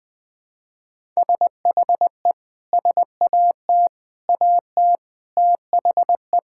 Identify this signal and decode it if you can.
Morse code, heard as a keyed tone, about 20 wpm, 700 Hz. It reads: SHE SAT AT THE
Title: Sentences with 4-Word Phrases - 20wpm